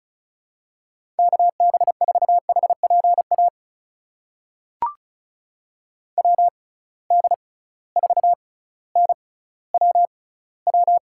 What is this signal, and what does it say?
Morse code, heard as a keyed tone, about 36 wpm, spaced as for 18 wpm, 700 Hz. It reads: KB4HPA WD4NWW
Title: USA Call Signs - 6 Characters - 6x Character Spacing - 35wpm